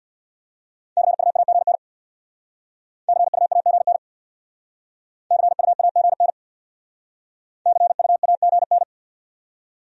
Morse code, heard as a keyed tone, about 64 wpm, 700 Hz. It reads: CUAGN CUAGN CUAGN CUAGN